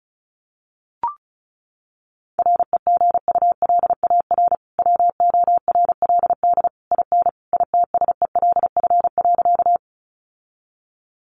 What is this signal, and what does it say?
Morse code, heard as a keyed tone, about 34 wpm, 700 Hz. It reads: REGULAR WORLD IN ITSELF.
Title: Sentences from Top 900 Words 35wpm